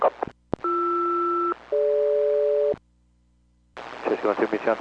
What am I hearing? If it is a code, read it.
Aircraft SELCAL, heard as two pairs of tones, with speech. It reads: BR-DG